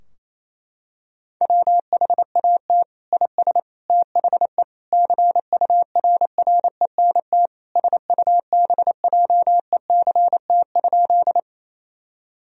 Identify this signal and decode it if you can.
Morse code, heard as a keyed tone, about 28 wpm, 700 Hz. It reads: WHAT IS THE CURRENT SUBJECT?
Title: Sentences from Top 1000 Words - Review Part 4 - 28wpm